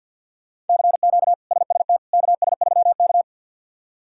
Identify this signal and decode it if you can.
Morse code, heard as a keyed tone, about 46 wpm, 700 Hz. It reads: CQ SST KS3K
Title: K1USN SST QSO Practice - 50wpm